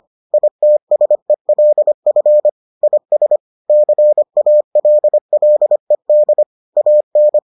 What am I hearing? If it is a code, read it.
Morse code, heard as a keyed tone, about 25 wpm, 600 Hz. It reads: ITSELF IS CALLED AN